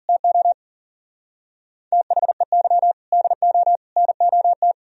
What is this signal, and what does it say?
Morse code, heard as a keyed tone, about 52 wpm, 700 Hz. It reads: TO THEY DO NOT